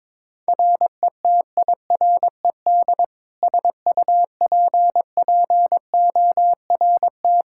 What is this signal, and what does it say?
Morse code, heard as a keyed tone, about 22 wpm, 700 Hz. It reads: RETIRED SUPPORT
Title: K1USN SST QSO Practice 22wpm